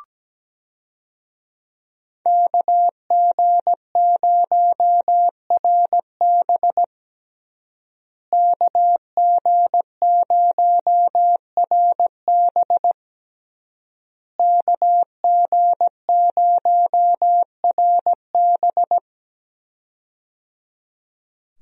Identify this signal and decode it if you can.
Morse code, heard as a keyed tone, about 17 wpm, 700 Hz. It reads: KG0RB KG0RB KG0RB
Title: USA Call Signs - 5 Characters - Repeated 3x - 17wpm